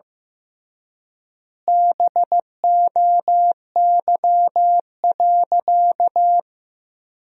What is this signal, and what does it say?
Morse code, heard as a keyed tone, about 15 wpm, 700 Hz. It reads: BOY.
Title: 3x Word-Spacing - Sentences from Top 200 Words - 15wpm